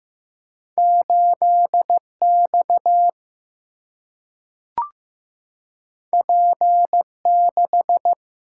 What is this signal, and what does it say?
Morse code, heard as a keyed tone, about 15 wpm, 700 Hz. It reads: 8X P6
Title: Single-Letter Single-Number 15wpm